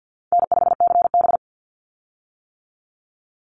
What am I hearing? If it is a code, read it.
Morse code, heard as a keyed tone, about 48 wpm, 700 Hz. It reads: N5CB
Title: USA Call Signs 50wpm